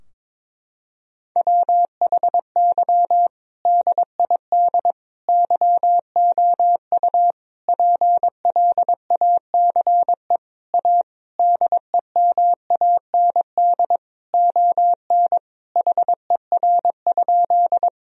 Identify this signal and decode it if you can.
Morse code, heard as a keyed tone, about 22 wpm, 700 Hz. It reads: WHY DID YOU PLACE A DEMAND ON HER?
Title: Sentences from Top 1000 Words - Review Part 4 - 22wpm